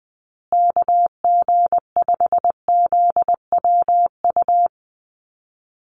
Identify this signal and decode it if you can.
Morse code, heard as a keyed tone, about 20 wpm, 700 Hz. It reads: KG5ZWU
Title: USA Call Signs Repeated 3x 20wpm